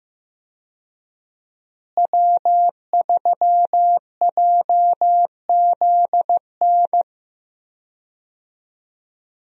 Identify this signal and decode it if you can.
Morse code, heard as a keyed tone, about 15 wpm, 700 Hz. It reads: W3JZN
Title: USA Call Signs - 5 Characters - Repeated 2x - 15wpm